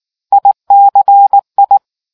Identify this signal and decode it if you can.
Morse code, heard as a keyed tone, about 18 wpm, 800 Hz. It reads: ICI